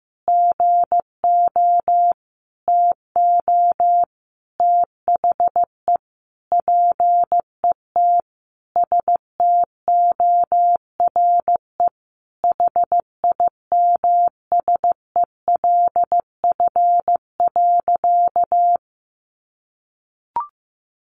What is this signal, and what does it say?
Morse code, heard as a keyed tone, about 15 wpm, 700 Hz. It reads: GO TO THE PET STORE HIMSELF.
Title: Additional Easy Phrases 15wpm